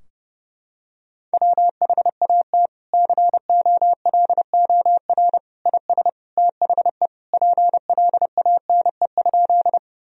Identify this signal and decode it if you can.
Morse code, heard as a keyed tone, about 31 wpm, 700 Hz. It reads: WHAT COLOR IS THE PLANE?
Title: Sentences from Top 1000 Words - Review Part 3 - 30wpm